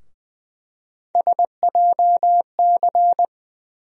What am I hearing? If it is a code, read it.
Morse code, heard as a keyed tone, about 20 wpm, 700 Hz. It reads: SJC
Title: USA Call Sign Postfixes - 3 Letters 20wpm